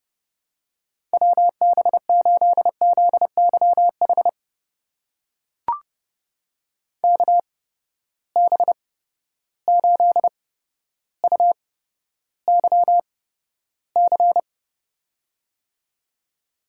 Morse code, heard as a keyed tone, about 30 wpm, spaced as for 13 wpm, 700 Hz. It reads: WB8ZYH KB8UYC
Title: USA Call Signs - 6 Characters - 8x Character Spacing - 30wpm